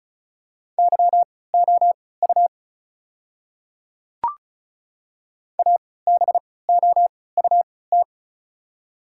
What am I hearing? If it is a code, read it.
Morse code, heard as a keyed tone, about 34 wpm, spaced as for 20 wpm, 700 Hz. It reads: YOU ABOUT
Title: 3x Character Spacing - Top 200 Words - 35wpm